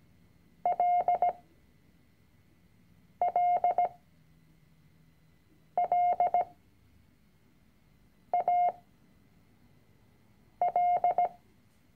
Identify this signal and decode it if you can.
Morse code, heard as a keyed tone, about 17 wpm, spaced as for 4 wpm, 710 Hz. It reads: LLLAL
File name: CW-Online-Kurs / Anfänger / Lektion 006